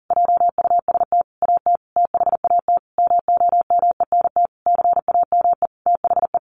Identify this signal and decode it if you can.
Morse code, heard as a keyed tone, about 40 wpm, 700 Hz. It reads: JUST AT THAT MOMENT CAME THE
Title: Sentences from Top 1500 Words 40wpm